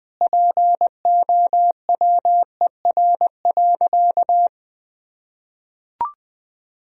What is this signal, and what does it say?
Morse code, heard as a keyed tone, about 20 wpm, 700 Hz. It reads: POWER.
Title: Sentences from Top 1000 Words - Review Part 4 - 20wpm